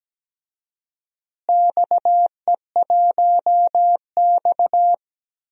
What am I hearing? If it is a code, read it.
Morse code, heard as a keyed tone, about 17 wpm, 700 Hz. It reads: XE1X
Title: DXCC-100 Call Signs - 4 Characters - Repeated 3x - 17wpm